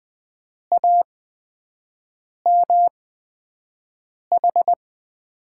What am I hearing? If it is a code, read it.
Morse code, heard as a keyed tone, about 20 wpm, spaced as for 5 wpm, 700 Hz. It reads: AMH
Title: USA Call Signs - 5 Characters - 8x Character Spacing - 20wpm